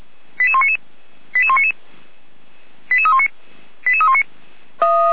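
Five-tone selective call (ZVEI), heard as quick runs of five tones, tones of about 70 ms. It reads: 88199 88199 88319 88319
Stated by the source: ZVEI five-tone selective call received by SDR.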